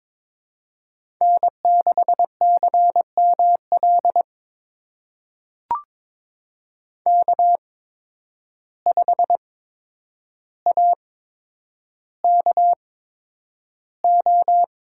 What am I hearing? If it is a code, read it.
Morse code, heard as a keyed tone, about 22 wpm, spaced as for 9 wpm, 700 Hz. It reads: N6CML K5AKO
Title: USA Call Signs - 8x Character Spacing - 22wpm